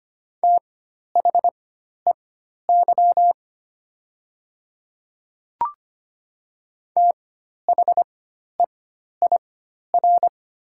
Morse code, heard as a keyed tone, about 25 wpm, spaced as for 12 wpm, 700 Hz. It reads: THEY THEIR
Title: 4x Character Spacing - Top 100 Words - 25wpm